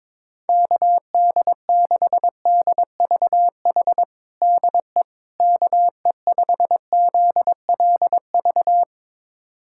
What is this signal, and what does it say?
Morse code, heard as a keyed tone, about 22 wpm, 700 Hz. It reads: KD6DVH DE KE5ZLV
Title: USA Callsign DE Callsign 22wpm